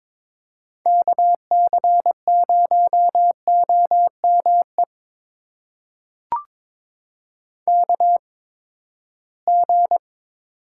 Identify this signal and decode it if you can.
Morse code, heard as a keyed tone, about 22 wpm, spaced as for 15 wpm, 700 Hz. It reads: KC0OME KG
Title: USA Call Signs - 6 Characters - 8x Character Spacing - 22wpm